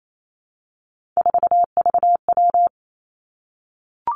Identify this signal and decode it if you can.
Morse code, heard as a keyed tone, about 28 wpm, 700 Hz. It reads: 4VW
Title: Three Challenging Letter-Number Combinations 28wpm